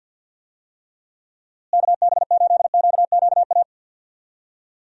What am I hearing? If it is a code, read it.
Morse code, heard as a keyed tone, about 51 wpm, 700 Hz. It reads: KB8QQA
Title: USA Call Signs - 6 Characters - Repeated 2x - 50wpm